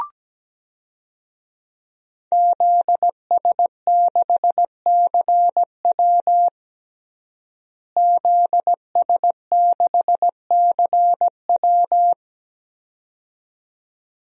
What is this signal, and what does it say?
Morse code, heard as a keyed tone, about 17 wpm, 700 Hz. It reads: ZS6CW ZS6CW
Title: DXCC-100 Call Signs - 5 Characters - Repeated 2x - 17wpm